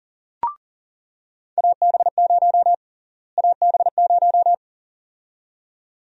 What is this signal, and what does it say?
Morse code, heard as a keyed tone, about 40 wpm, 700 Hz. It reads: AB0 AB0
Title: USA Call Sign Prefixes - Repeated 2x - 40wpm